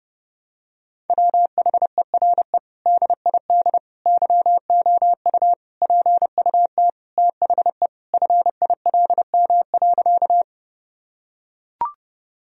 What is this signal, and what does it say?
Morse code, heard as a keyed tone, about 30 wpm, 700 Hz. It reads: WHERE DID YOU PUT THE FILM.
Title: Sentences from Top 1000 Words - Review Part 3 - 30wpm